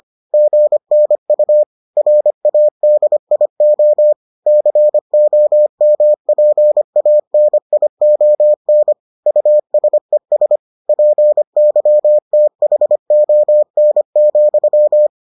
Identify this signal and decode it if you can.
Morse code, heard as a keyed tone, about 25 wpm, 600 Hz. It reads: GNU RADIO COMPANION USES PYTHON,